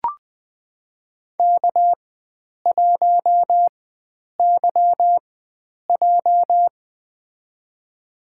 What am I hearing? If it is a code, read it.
Morse code, heard as a keyed tone, about 20 wpm, spaced as for 9 wpm, 700 Hz. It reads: K1YJ
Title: USA Call Signs - 4 Characters - 4x Character Spacing - 20wpm